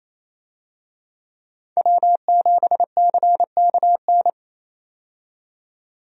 Morse code, heard as a keyed tone, about 28 wpm, 700 Hz. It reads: W7CKN
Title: USA Call Signs - 5 Characters - 28wpm